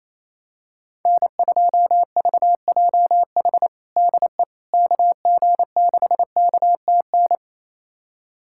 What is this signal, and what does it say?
Morse code, heard as a keyed tone, about 28 wpm, 700 Hz. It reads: N2VJH DE KG6KTN
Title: USA Callsign DE Callsign 28wpm